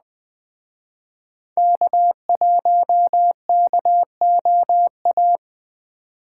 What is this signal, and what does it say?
Morse code, heard as a keyed tone, about 21 wpm, 700 Hz. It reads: K1KOA